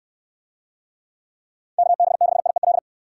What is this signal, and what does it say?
Morse code, heard as a keyed tone, about 66 wpm, 700 Hz. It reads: KB6IEL